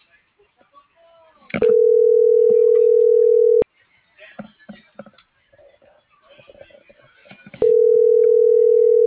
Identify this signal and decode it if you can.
Ringback tone (North America), 440 + 480 Hz.
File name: CENTRAL MAERSK / out sandovalmsk - .64752